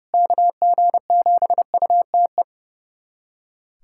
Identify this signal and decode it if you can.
Morse code, heard as a keyed tone, about 32 wpm, 700 Hz. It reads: KG7UTE
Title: USA Callsign DE Callsign - 6 Characters - 30wpm